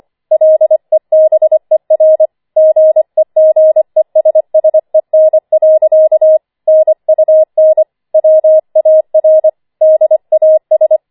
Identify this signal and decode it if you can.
Morse code, heard as a keyed tone, about 25 wpm, 630 Hz. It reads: LEBER GEGESSEN. NUN WAR DAS
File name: 25wpm